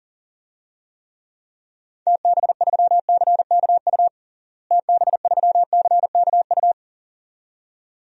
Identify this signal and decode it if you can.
Morse code, heard as a keyed tone, about 40 wpm, 700 Hz. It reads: TB3CKU TB3CKU
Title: DXCC-100 Call Signs - 6 Characters - Repeated 2x - 40wpm